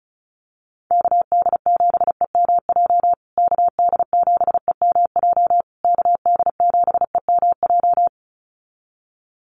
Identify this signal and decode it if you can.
Morse code, heard as a keyed tone, about 34 wpm, 700 Hz. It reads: KD7EMJ KD7EMJ KD7EMJ
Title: USA Call Signs Repeated 3x 35wpm